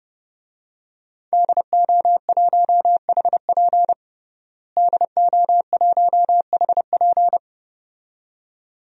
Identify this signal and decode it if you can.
Morse code, heard as a keyed tone, about 30 wpm, 700 Hz. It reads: DO1HP DO1HP
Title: DXCC-100 Call Signs - 5 Characters - Repeated 2x - 30wpm